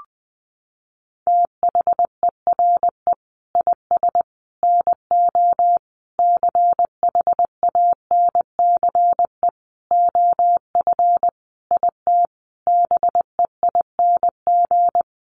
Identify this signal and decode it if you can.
Morse code, heard as a keyed tone, about 20 wpm, 700 Hz. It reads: THERE IS NO CHANCE OF IT BEING
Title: Sentences from Top 800 Words 20wpm